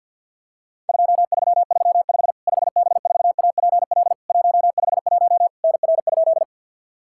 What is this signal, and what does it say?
Morse code, heard as a keyed tone, about 50 wpm, 700 Hz. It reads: 2335 5B4APL 151 NR?
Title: Worked All Europe DX Contest - QTC Exchange with Repeats - 50wpm